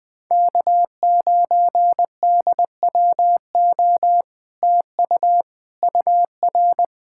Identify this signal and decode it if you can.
Morse code, heard as a keyed tone, about 20 wpm, 700 Hz. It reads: K9DWO TU UR
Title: Speed-Racing Common QSO Phrases - 20wpm